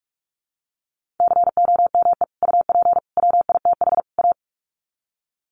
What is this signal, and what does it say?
Morse code, heard as a keyed tone, about 44 wpm, 700 Hz. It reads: COME UP WITH A
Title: Sets of 4 Words 45wpm